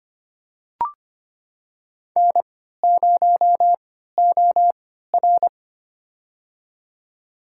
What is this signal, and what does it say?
Morse code, heard as a keyed tone, about 25 wpm, spaced as for 14 wpm, 700 Hz. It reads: N0OR
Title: USA Call Signs - 4 Characters - 3x Character Spacing - 25wpm